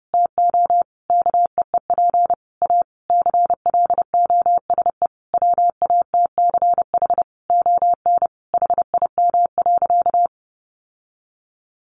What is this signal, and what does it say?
Morse code, heard as a keyed tone, about 30 wpm, 700 Hz. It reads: TO KEEP A CLOSE WATCH ON HIM.
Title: Sentences from Top 1300 Words 30wpm